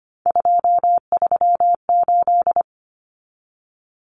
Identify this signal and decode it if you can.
Morse code, heard as a keyed tone, about 25 wpm, 700 Hz. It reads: 238
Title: Common Cut Numbers - 3-Digit 25wpm